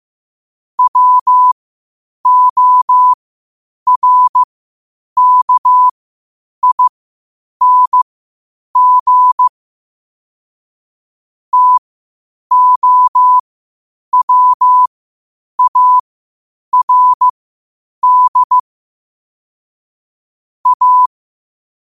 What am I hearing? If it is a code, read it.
Morse code, heard as a keyed tone, about 15 wpm, spaced as for 8 wpm, 1000 Hz. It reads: WORKING TOWARD A
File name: qotdfiles / 8wpm